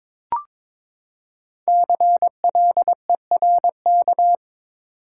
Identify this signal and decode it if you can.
Morse code, heard as a keyed tone, about 22 wpm, 700 Hz. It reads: CLERK
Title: Five Letter Words - 22wpm